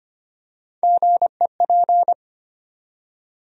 Morse code, heard as a keyed tone, about 25 wpm, 700 Hz. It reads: GEP
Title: USA Call Sign Postfixes - 3 Letters 25wpm